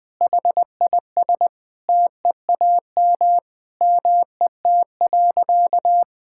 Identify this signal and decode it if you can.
Morse code, heard as a keyed tone, about 20 wpm, 700 Hz. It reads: HIS TEAM MET.
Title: Sentences with 5-Word Phrases - 20wpm